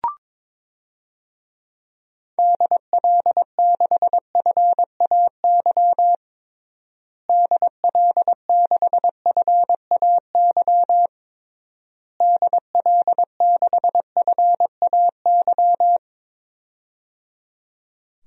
Morse code, heard as a keyed tone, about 22 wpm, 700 Hz. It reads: DL6FAY DL6FAY DL6FAY
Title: DXCC-100 Call Signs - 6 Characters - Repeated 3x - 22wpm